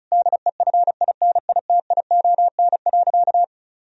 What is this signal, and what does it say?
Morse code, heard as a keyed tone, about 34 wpm, 700 Hz. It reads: DEFINITION.
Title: Sentences from Top 1000 Words - Review Part 4 - 35wpm